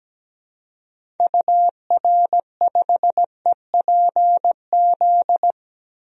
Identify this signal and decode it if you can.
Morse code, heard as a keyed tone, about 17 wpm, 700 Hz. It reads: UR5EPZ
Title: DXCC-100 Call Signs - 6 Characters - Repeated 2x - 17wpm